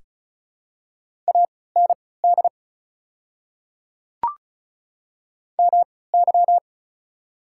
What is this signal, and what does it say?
Morse code, heard as a keyed tone, about 36 wpm, spaced as for 20 wpm, 700 Hz. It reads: AND MY
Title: 3x Character Spacing - Top 100 Words - 35wpm